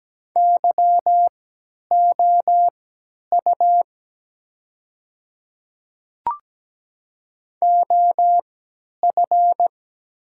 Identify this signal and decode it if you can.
Morse code, heard as a keyed tone, about 17 wpm, spaced as for 10 wpm, 700 Hz. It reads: YOU OF
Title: 3x Character Spacing - Top 100 Words - 17wpm